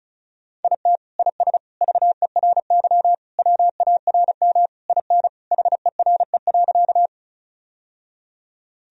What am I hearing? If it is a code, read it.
Morse code, heard as a keyed tone, about 36 wpm, 700 Hz. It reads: IT IS VERY WARM IN HERE.
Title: Sentences from Top 1000 Words - Review Part 2 - 35wpm